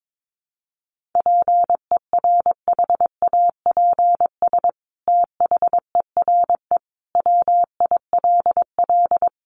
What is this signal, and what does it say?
Morse code, heard as a keyed tone, about 22 wpm, 700 Hz. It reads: PERHAPS THERE WILL
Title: Sentences from Top 1400 Words 22wpm